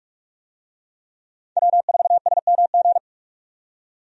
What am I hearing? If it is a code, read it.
Morse code, heard as a keyed tone, about 46 wpm, 700 Hz. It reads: W4SMG
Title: USA Call Signs - 5 Characters - 45wpm